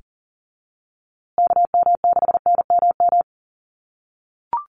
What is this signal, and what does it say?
Morse code, heard as a keyed tone, about 40 wpm, 700 Hz. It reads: KM6NMM
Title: USA Call Signs Repeated 2x 40wpm